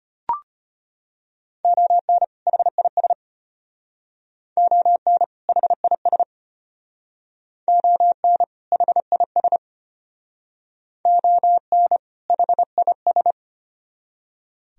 Morse code, heard as a keyed tone, about 34 wpm, 700 Hz. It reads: ON HIS ON HIS ON HIS ON HIS